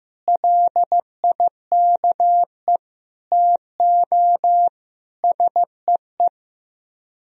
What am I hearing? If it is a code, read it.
Morse code, heard as a keyed tone, about 15 wpm, 700 Hz. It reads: LIKE TO SEE
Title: Speed-Racing - Sets of 5 Words 15wpm